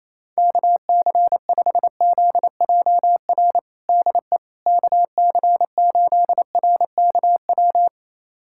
Morse code, heard as a keyed tone, about 28 wpm, 700 Hz. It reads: KC5ZJR DE KC8RKW
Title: USA Callsign DE Callsign - 6 Characters - 28wpm